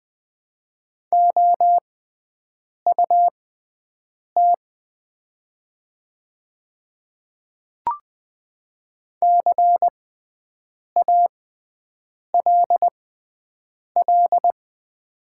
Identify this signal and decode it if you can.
Morse code, heard as a keyed tone, about 20 wpm, spaced as for 7 wpm, 700 Hz. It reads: OUT CALL
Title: 6x Character Spacing - Top 200 Words - 20wpm